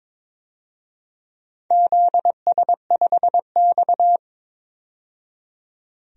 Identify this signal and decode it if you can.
Morse code, heard as a keyed tone, about 22 wpm, 700 Hz. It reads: ZS5X
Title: DXCC-100 Call Signs - 4 Characters - 22wpm